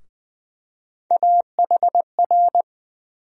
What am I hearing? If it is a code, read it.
Morse code, heard as a keyed tone, about 20 wpm, 700 Hz. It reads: AHR
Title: USA Call Sign Postfixes - 3 Letters 20wpm